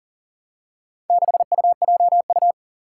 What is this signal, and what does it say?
Morse code, heard as a keyed tone, about 40 wpm, 700 Hz. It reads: 6UJU
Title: Four Challenging Letter-Number Combinations 40wpm